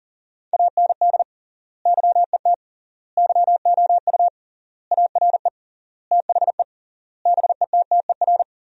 Morse code, heard as a keyed tone, about 40 wpm, 700 Hz. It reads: AND YET YOU ARE THE BETTER
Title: 3x Word-Spacing - Sentences from Top 500 Words - 40wpm